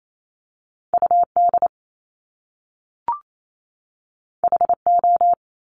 Morse code, heard as a keyed tone, about 28 wpm, 700 Hz. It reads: UD HO
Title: Top Two Letter Combinations in Words 28wpm